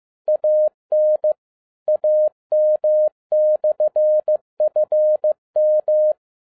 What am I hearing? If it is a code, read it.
Morse code, heard as a keyed tone, about 15 wpm, 610 Hz. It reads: AN AM/FM
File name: 2012-01-mp3 / n07-ARA JR0DIL-15wpm